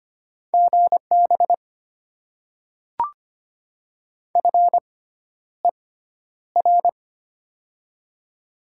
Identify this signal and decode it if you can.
Morse code, heard as a keyed tone, about 25 wpm, spaced as for 11 wpm, 700 Hz. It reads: GB FER
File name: Top 90 QSO Elements - 6x Character Spacing - 25wpm